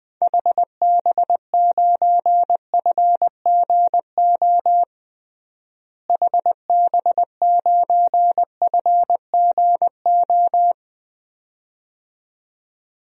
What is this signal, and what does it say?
Morse code, heard as a keyed tone, about 20 wpm, 700 Hz. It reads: HB9FGO HB9FGO
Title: DXCC-100 Call Signs - 6 Characters - Repeated 2x - 20wpm